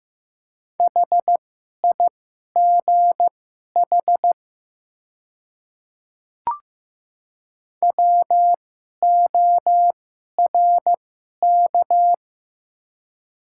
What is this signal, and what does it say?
Morse code, heard as a keyed tone, about 15 wpm, spaced as for 11 wpm, 700 Hz. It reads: HIGH WORK
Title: 2x Character Spacing - Top 200 Words - 15wpm